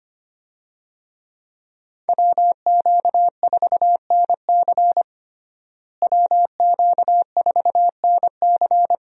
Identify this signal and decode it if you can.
Morse code, heard as a keyed tone, about 25 wpm, 700 Hz. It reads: WQ4NC WQ4NC
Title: USA Call Signs - 5 Characters - Repeated 3x - 25wpm